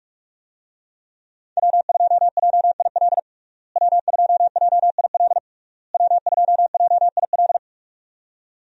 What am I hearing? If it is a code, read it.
Morse code, heard as a keyed tone, about 46 wpm, 700 Hz. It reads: W2JIL W2JIL W2JIL
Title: USA Call Signs - 5 Characters - Repeated 3x - 45wpm